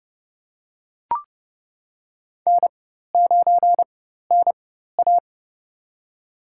Morse code, heard as a keyed tone, about 32 wpm, spaced as for 14 wpm, 700 Hz. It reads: N9NA
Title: USA Call Signs - 4 Characters - 4x Character Spacing - 30wpm